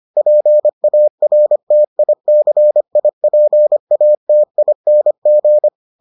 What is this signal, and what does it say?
Morse code, heard as a keyed tone, about 25 wpm, 600 Hz. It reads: PARTICIPATING